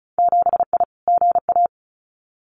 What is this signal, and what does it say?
Morse code, heard as a keyed tone, about 36 wpm, 700 Hz. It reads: 7I GA
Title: Winter Field Day Exchanges Encore 35wpm